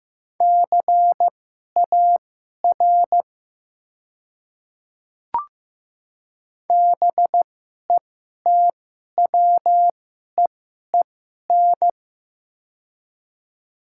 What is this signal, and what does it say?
Morse code, heard as a keyed tone, about 15 wpm, spaced as for 11 wpm, 700 Hz. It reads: CAR BETWEEN
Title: 2x Character Spacing - Top 300 Words - 15wpm